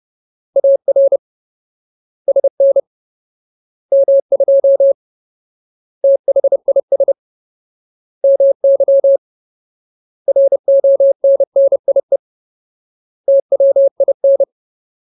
Morse code, heard as a keyed tone, about 30 wpm, 550 Hz. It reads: AR SN M2 THIS MY RONNIE TWIN
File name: Ultimate Word List / 30WPM Ultimate